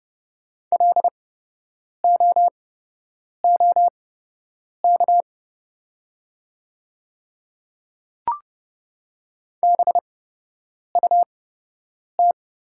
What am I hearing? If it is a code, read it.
Morse code, heard as a keyed tone, about 30 wpm, spaced as for 8 wpm, 700 Hz. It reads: LOOK BUT
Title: 8x Character Spacing - Top 200 Words - 30wpm